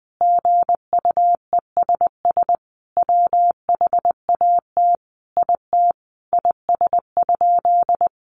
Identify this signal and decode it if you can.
Morse code, heard as a keyed tone, about 20 wpm, 700 Hz. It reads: GUESS WHAT IT IS?
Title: Sentences from Top 1500 Words 20wpm